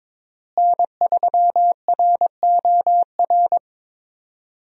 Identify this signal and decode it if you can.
Morse code, heard as a keyed tone, about 22 wpm, 700 Hz. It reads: N3ROR
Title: USA Call Signs - 5 Characters - Repeated 2x - 22wpm